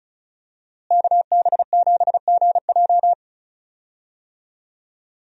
Morse code, heard as a keyed tone, about 34 wpm, 700 Hz. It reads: KB7GJ
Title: USA Call Signs - 5 Characters - Repeated 3x - 35wpm